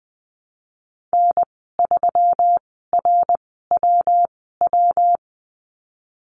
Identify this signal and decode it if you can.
Morse code, heard as a keyed tone, about 20 wpm, spaced as for 15 wpm, 700 Hz. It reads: N3RWW
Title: 140 Lesson 23g - Letter f - Call Signs 20wpm